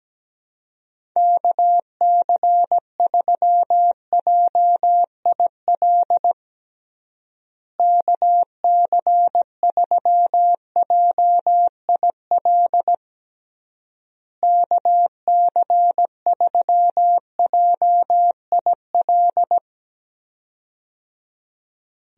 Morse code, heard as a keyed tone, about 17 wpm, 700 Hz. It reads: KC3JIL KC3JIL KC3JIL
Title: USA Call Signs - 6 Characters - Repeated 3x - 17wpm